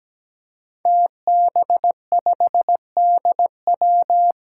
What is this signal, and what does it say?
Morse code, heard as a keyed tone, about 17 wpm, 700 Hz. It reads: TB5DW
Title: Five Challenging Letter-Number Combinations 17wpm